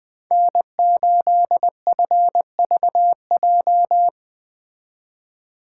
Morse code, heard as a keyed tone, about 19 wpm, 700 Hz. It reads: N8FVJ